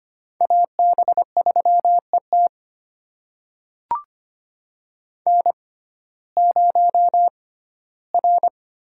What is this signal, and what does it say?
Morse code, heard as a keyed tone, about 25 wpm, spaced as for 15 wpm, 700 Hz. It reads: AB3ET N0R
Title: USA Call Signs - 5 Characters - 6x Character Spacing - 25wpm